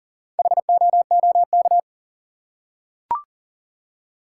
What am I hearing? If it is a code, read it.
Morse code, heard as a keyed tone, about 40 wpm, 700 Hz. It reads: HOOK
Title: Four Letter Words - 40wpm